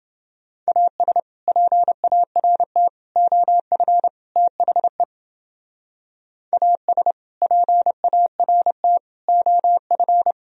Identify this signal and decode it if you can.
Morse code, heard as a keyed tone, about 32 wpm, 700 Hz. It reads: AS PART OF THE AS PART OF